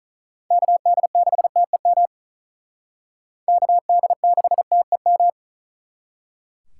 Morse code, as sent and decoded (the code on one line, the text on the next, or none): -.- -.. -.... - . -- / -.- -.. -.... - . --
KD6TEM KD6TEM